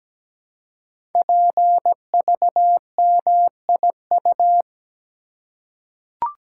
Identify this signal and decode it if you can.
Morse code, heard as a keyed tone, about 17 wpm, 700 Hz. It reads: PVMIU
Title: Code Groups - 5 Letter:Numbers - 17wpm